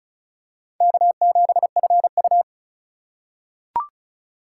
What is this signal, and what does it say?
Morse code, heard as a keyed tone, about 36 wpm, 700 Hz. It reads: K7FU
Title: USA Call Signs - 4 Characters - 3x Character Spacing - 35wpm